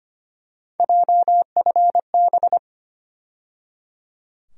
Morse code, heard as a keyed tone, about 25 wpm, 700 Hz. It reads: JFB
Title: USA Call Sign Postfixes - 3 Letters 25wpm